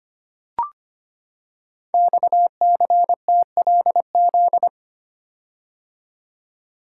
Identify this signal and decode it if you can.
Morse code, heard as a keyed tone, about 25 wpm, 700 Hz. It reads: XCTLZ
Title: Code Groups - 5 Letters - 25wpm